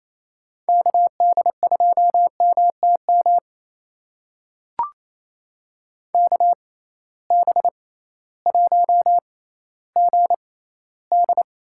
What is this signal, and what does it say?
Morse code, heard as a keyed tone, about 28 wpm, spaced as for 15 wpm, 700 Hz. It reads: KD2MTM KB1GD
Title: USA Call Signs - 6 Characters - 6x Character Spacing - 28wpm